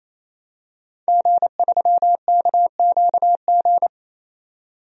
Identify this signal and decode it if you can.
Morse code, heard as a keyed tone, about 28 wpm, 700 Hz. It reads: G3KQG
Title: DXCC-100 Call Signs - 5 Characters - 28wpm